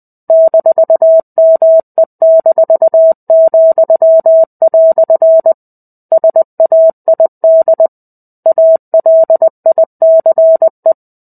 Morse code, heard as a keyed tone, about 20 wpm, 650 Hz. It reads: -ME-," SAID ALICE
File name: 05-Chapter-05 20 WPM